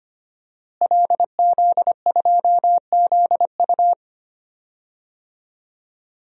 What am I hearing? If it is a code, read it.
Morse code, heard as a keyed tone, about 25 wpm, 700 Hz. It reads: LZ2ZU
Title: DXCC-100 Call Signs - 5 Characters - Repeated 2x - 25wpm